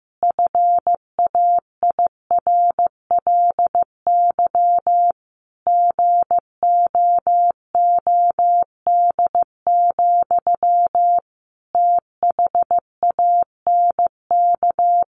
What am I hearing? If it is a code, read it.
Morse code, heard as a keyed tone, about 14 wpm, 700 Hz. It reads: FAIRLY GOOD, THANK
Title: Sentences from Top 1100 Words 15wpm